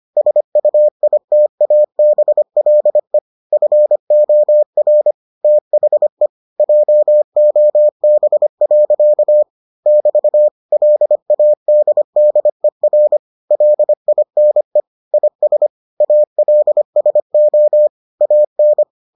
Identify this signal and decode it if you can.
Morse code, heard as a keyed tone, about 25 wpm, 600 Hz. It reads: SUITABLE FOR THE JOB. <BT> LADDER LINE IS ALSO AN